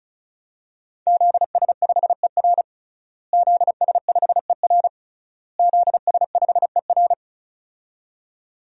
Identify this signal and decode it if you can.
Morse code, heard as a keyed tone, about 36 wpm, 700 Hz. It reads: ZS5ER ZS5ER ZS5ER
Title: DXCC-100 Call Signs - 5 Characters - Repeated 3x - 35wpm